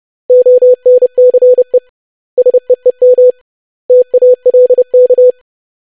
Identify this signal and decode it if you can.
Morse code, heard as a keyed tone, about 30 wpm, 500 Hz. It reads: ONCE SEEM TALK
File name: basic500-25wpm